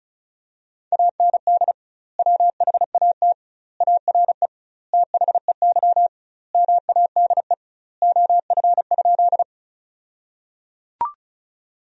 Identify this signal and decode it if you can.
Morse code, heard as a keyed tone, about 36 wpm, 700 Hz. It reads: AND WHAT ARE THEY MADE OF?
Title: 2x Word-Spacing - Sentences from Top 200 Words - 35wpm